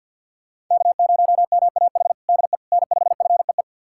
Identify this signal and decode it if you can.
Morse code, heard as a keyed tone, about 49 wpm, 700 Hz. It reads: K0MAH DE N5FEE
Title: USA Callsign DE Callsign 50wpm